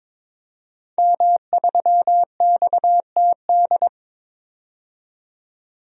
Morse code, heard as a keyed tone, about 21 wpm, 700 Hz. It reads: M3XTD
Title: DXCC-100 Call Signs - 5 Characters - 22wpm